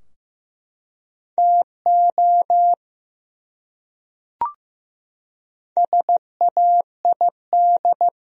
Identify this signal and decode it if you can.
Morse code, heard as a keyed tone, about 15 wpm, 700 Hz. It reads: TO SAID
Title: Top 100 Words Repeated 15wpm